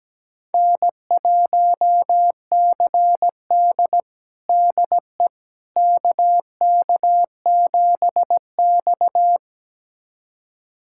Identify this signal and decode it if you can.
Morse code, heard as a keyed tone, about 17 wpm, 700 Hz. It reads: N1CD DE KK7X
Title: USA Callsign DE Callsign - 4 Characters - 17wpm